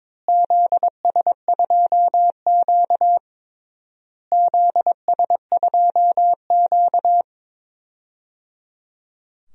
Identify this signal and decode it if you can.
Morse code, heard as a keyed tone, about 22 wpm, 700 Hz. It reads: ZS2Q ZS2Q
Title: DXCC-100 Call Signs - 4 Characters - Repeated 3x - 22wpm